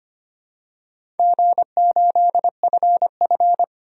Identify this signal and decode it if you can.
Morse code, heard as a keyed tone, about 25 wpm, 700 Hz. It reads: G8FF
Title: DXCC-100 Call Signs - 4 Characters - Repeated 2x - 25wpm